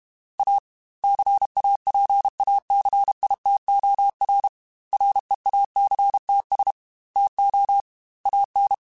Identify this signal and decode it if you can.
Morse code, heard as a keyed tone, about 32 wpm, 800 Hz. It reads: A CAPACITOR REACTS TO AN
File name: Technical Subjects / Technical Subjects-mp3s / XC @30